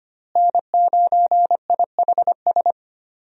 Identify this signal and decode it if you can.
Morse code, heard as a keyed tone, about 25 wpm, 700 Hz. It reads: N9IHS
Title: USA Call Signs - 5 Characters - Repeated 3x - 25wpm